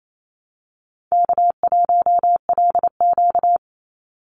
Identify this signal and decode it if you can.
Morse code, heard as a keyed tone, about 28 wpm, 700 Hz. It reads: K1LQ
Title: USA Call Signs 28wpm